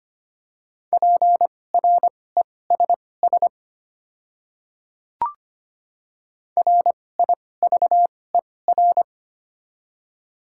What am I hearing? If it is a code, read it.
Morse code, heard as a keyed tone, about 25 wpm, spaced as for 18 wpm, 700 Hz. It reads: PRESS RIVER
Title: 2x Character Spacing - Top 300 Words - 25wpm